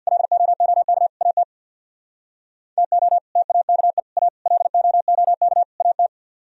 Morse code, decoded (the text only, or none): LOOK AT TO TAKE A LOOK AT